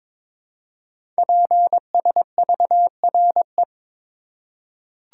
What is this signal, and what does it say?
Morse code, heard as a keyed tone, about 23 wpm, 700 Hz. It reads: PSVRE
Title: Code Groups - 5 Letter:Numbers - 22wpm